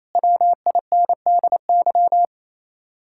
Morse code, heard as a keyed tone, about 33 wpm, 700 Hz. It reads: WINDY
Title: Speed-Racing Common QSO Phrases Encore - 28wpm